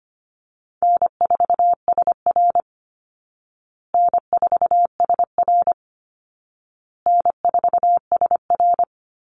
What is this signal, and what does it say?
Morse code, heard as a keyed tone, about 25 wpm, 700 Hz. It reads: N4SR N4SR N4SR
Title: Call Signs Repeated 2x - 25wpm